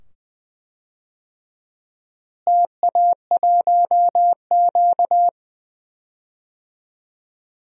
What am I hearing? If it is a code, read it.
Morse code, heard as a keyed tone, about 20 wpm, 700 Hz. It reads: TA1Q
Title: DXCC-100 Call Signs - 4 Characters - Repeated 2x - 20wpm